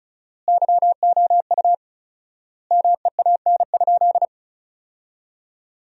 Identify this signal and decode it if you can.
Morse code, heard as a keyed tone, about 36 wpm, 700 Hz. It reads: YOU MEAN?
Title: 4x Word-Spacing - Sentences from Top 500 Words - 35wpm